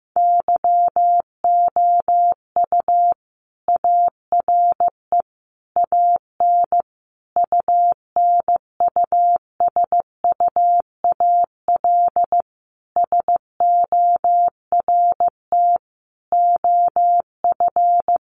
Morse code, heard as a keyed tone, about 15 wpm, 700 Hz. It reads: YOU ARE AN UNUSUAL SORT OF
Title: Sentences from Top 1400 Words 15wpm